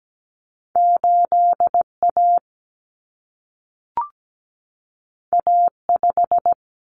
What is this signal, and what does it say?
Morse code, heard as a keyed tone, about 17 wpm, 700 Hz. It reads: 8A A5
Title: Common Cut Numbers - 2-Digit 17wpm